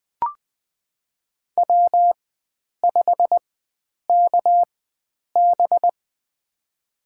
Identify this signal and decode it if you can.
Morse code, heard as a keyed tone, about 20 wpm, spaced as for 9 wpm, 700 Hz. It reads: W5KB
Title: USA Call Signs - 4x Character Spacing - 20wpm